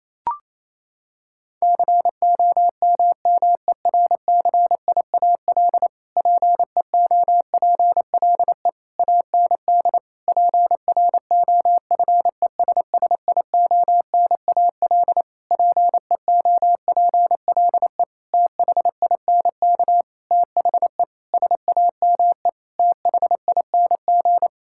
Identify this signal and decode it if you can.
Morse code, heard as a keyed tone, about 28 wpm, 700 Hz. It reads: COMMERCIAL PEOPLE AND PROFESSIONAL PEOPLE THINK THE SAME THING
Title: Sentences from Top 1000 Words - Review Part 5 - 28wpm